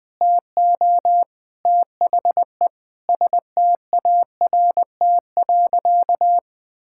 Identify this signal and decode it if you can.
Morse code, heard as a keyed tone, about 20 wpm, 700 Hz. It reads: TO THE START.
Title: Sentences with 3-Word Phrases - 20wpm